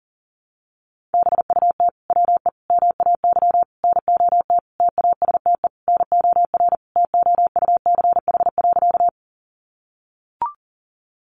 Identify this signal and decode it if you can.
Morse code, heard as a keyed tone, about 40 wpm, 700 Hz. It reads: BUT WE MAY NOT TASTE NOR TOUCH.
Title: Sentences from Top 1500 Words 40wpm